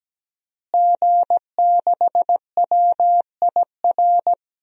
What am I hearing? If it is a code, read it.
Morse code, heard as a keyed tone, about 17 wpm, 700 Hz. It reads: G6WIR
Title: DXCC-100 Call Signs - 5 Characters - Repeated 2x - 17wpm